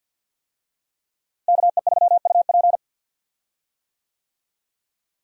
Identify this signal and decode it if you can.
Morse code, heard as a keyed tone, about 51 wpm, 690 Hz. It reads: KE3UP
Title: USA Call Signs - 5 Characters - Repeated 3x - 50wpm